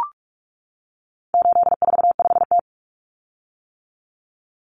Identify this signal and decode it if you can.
Morse code, heard as a keyed tone, about 44 wpm, 700 Hz. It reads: 845T
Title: Common Cut Numbers - 4-Digit 45wpm